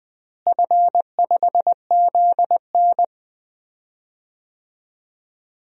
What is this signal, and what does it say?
Morse code, heard as a keyed tone, about 20 wpm, 700 Hz. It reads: F5ZN
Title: DXCC-100 Call Signs - 4 Characters - Repeated 3x - 20wpm